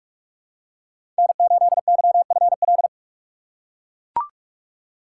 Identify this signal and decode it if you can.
Morse code, heard as a keyed tone, about 42 wpm, 700 Hz. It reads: N8YFL
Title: USA Call Signs - 5 Characters - 3x Character Spacing - 45wpm